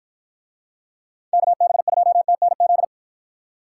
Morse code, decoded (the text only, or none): KB2TNZ